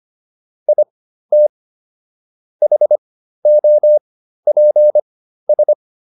Morse code, heard as a keyed tone, about 25 wpm, spaced as for 13 wpm, 600 Hz. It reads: IT HOPS